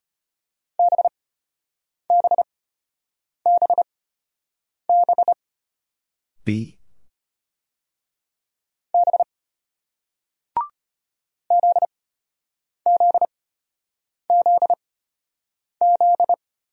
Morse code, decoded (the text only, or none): BBBB B ZZZZ